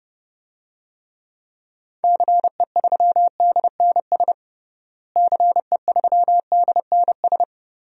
Morse code, heard as a keyed tone, about 30 wpm, 700 Hz. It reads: CE3DNS CE3DNS
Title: DXCC-100 Call Signs - 6 Characters - Repeated 2x - 30wpm